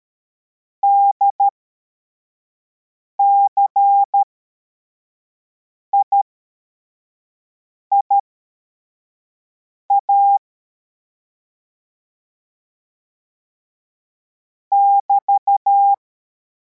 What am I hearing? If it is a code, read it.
Morse code, heard as a keyed tone, about 13 wpm, spaced as for 4 wpm, 800 Hz. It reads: DCIIA <BT>